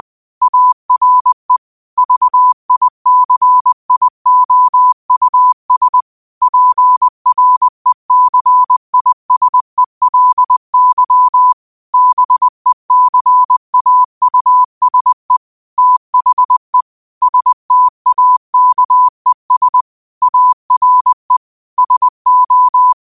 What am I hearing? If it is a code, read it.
Morse code, heard as a keyed tone, about 20 wpm, 1000 Hz. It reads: ARE VICIOUS PRECISELY BECAUSE THE STAKES ARE SO